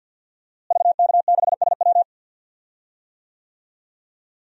Morse code, heard as a keyed tone, about 52 wpm, 700 Hz. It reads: VK6SW
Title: DXCC-100 Call Signs - 5 Characters - Repeated 2x - 50wpm